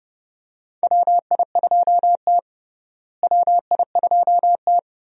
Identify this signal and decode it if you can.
Morse code, heard as a keyed tone, about 30 wpm, 700 Hz. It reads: WI2T WI2T
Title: USA Call Signs - 4 Characters - Repeated 2x - 30wpm